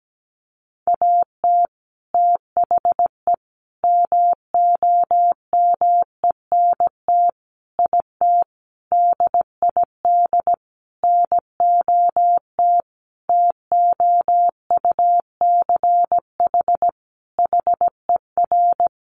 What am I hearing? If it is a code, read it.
Morse code, heard as a keyed tone, about 17 wpm, 700 Hz. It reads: AT THE MOMENT IT DID NOT TOUCH HER
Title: Sentences from Top 1200 Words 17wpm